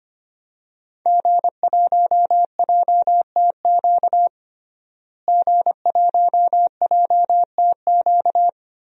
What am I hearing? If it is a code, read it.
Morse code, heard as a keyed tone, about 25 wpm, 700 Hz. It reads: G1JTQ G1JTQ
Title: DXCC-100 Call Signs - 5 Characters - Repeated 3x - 25wpm